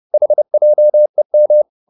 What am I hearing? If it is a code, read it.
Morse code, heard as a keyed tone, about 30 wpm, 600 Hz. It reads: HJEM
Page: Ferie 30 wpm